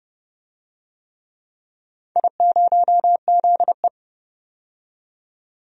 Morse code, heard as a keyed tone, about 31 wpm, 700 Hz. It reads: I0ZE
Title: DXCC-100 Call Signs - 4 Characters - 30wpm